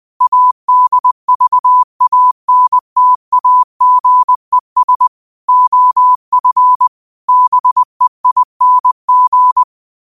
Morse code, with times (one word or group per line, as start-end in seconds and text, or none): 0.2-5.1 ADVANTAGES
5.5-6.9 OF
7.3-9.6 BEING